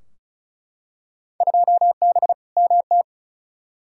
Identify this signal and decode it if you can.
Morse code, heard as a keyed tone, about 36 wpm, 700 Hz. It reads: 2B MT
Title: Field Day Exchanges Encore - 35wpm